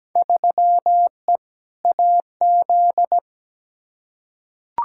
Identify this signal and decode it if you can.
Morse code, heard as a keyed tone, about 17 wpm, 700 Hz. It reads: 3E AZ
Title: Field Day Exchanges Encore - 17wpm